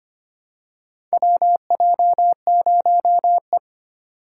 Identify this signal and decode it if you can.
Morse code, heard as a keyed tone, about 25 wpm, 700 Hz. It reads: WJ0E
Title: USA Call Signs - 4 Characters - 2x Character Spacing - 25wpm